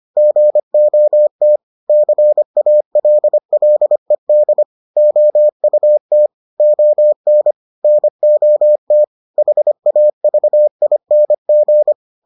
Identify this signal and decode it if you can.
Morse code, heard as a keyed tone, about 25 wpm, 600 Hz. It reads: GOT CALLED OUT ON NOT HAVING